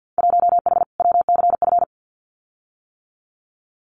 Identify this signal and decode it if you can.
Morse code, heard as a keyed tone, about 48 wpm, 700 Hz. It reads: 1H WCF
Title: Winter Field Day Exchanges 50wpm